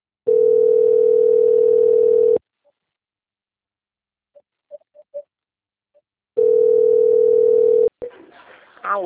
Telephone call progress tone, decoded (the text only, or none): ringback tone